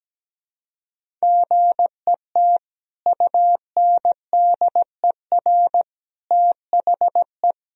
Morse code, heard as a keyed tone, about 17 wpm, 700 Hz. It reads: GET UNDER THE
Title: Sentences from Top 1000 Words - Review Part 1 - 17wpm